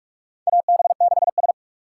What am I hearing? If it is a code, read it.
Morse code, heard as a keyed tone, about 46 wpm, 700 Hz. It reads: AB6S
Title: USA Call Signs - 2x Character Spacing - 45wpm